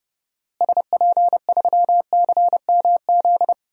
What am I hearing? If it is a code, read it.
Morse code, heard as a keyed tone, about 30 wpm, 700 Hz. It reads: SP3CMZ
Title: DXCC-100 Call Signs - 6 Characters - Repeated 2x - 30wpm